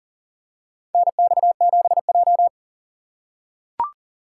Morse code, heard as a keyed tone, about 40 wpm, 700 Hz. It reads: NX7J
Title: USA Call Signs - 4 Characters - 2x Character Spacing - 40wpm